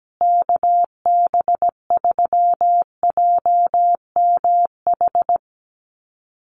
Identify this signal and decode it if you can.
Morse code, heard as a keyed tone, about 17 wpm, 700 Hz. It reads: KB3JMH
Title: USA Call Signs Repeated 3x 17wpm